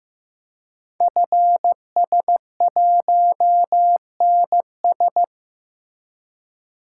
Morse code, heard as a keyed tone, about 15 wpm, 700 Hz. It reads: FS1NS
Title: Code Groups - 5 Letter:Numbers - 15wpm